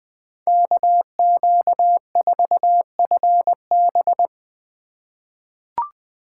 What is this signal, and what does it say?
Morse code, heard as a keyed tone, about 20 wpm, 700 Hz. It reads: KQ4FB
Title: USA Call Signs - 5 Characters - 3x Character Spacing - 20wpm